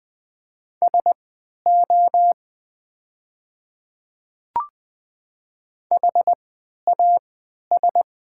Morse code, heard as a keyed tone, about 20 wpm, spaced as for 11 wpm, 700 Hz. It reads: SO HAS
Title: 3x Character Spacing - Top 100 Words - 20wpm